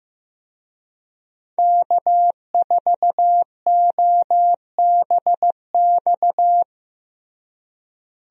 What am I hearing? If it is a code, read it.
Morse code, heard as a keyed tone, about 15 wpm, 700 Hz. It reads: K4OBX
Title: USA Call Signs - 5 Characters - Repeated 2x - 15wpm